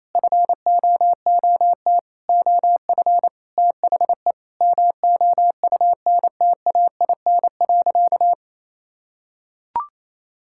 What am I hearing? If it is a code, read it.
Morse code, heard as a keyed tone, about 28 wpm, 700 Hz. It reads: FOOT OF THE MOUNTAIN.
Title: Sentences from Top 1000 Words - Review Part 5 - 28wpm